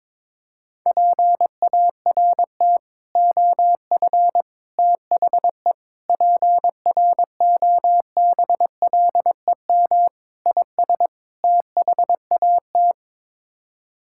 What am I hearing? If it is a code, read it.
Morse code, heard as a keyed tone, about 21 wpm, 700 Hz. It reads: PART OF THE PROBLEM IS THAT
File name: practice / 22wpm / Sets of 6 Words - 22wpm